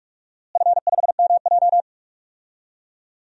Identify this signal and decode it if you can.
Morse code, heard as a keyed tone, about 46 wpm, 700 Hz. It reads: F5MJ
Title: DXCC-100 Call Signs - 4 Characters - Repeated 2x - 45wpm